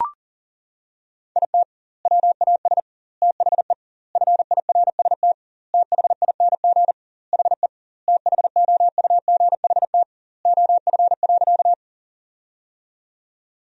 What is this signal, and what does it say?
Morse code, heard as a keyed tone, about 40 wpm, 700 Hz. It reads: IT WAS THE FIRST THING HE THOUGHT OF.
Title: 2x Word-Spacing - Sentences from Top 300 Words - 40wpm